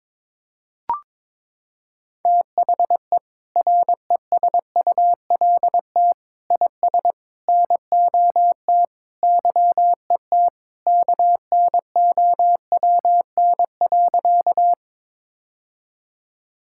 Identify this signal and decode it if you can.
Morse code, heard as a keyed tone, about 22 wpm, 700 Hz. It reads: THE RESULT IS NOT YET KNOWN.
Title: Sentences from Top 1000 Words - Review Part 5 - 22wpm